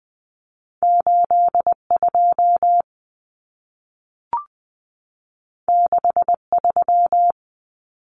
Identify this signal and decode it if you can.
Morse code, heard as a keyed tone, about 20 wpm, 700 Hz. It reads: 8 2 6 3
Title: Common Cut Numbers - 2-Digit 20wpm